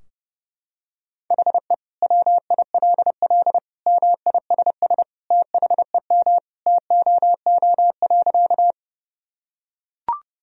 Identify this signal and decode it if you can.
Morse code, heard as a keyed tone, about 30 wpm, 700 Hz. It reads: HE WILL MISS THEM TOO.
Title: Sentences from Top 1000 Words - Review Part 3 - 30wpm